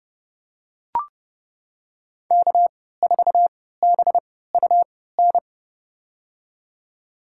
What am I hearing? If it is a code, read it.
Morse code, heard as a keyed tone, about 30 wpm, spaced as for 17 wpm, 700 Hz. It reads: K4BUN
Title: USA Call Signs - 3x Character Spacing - 30wpm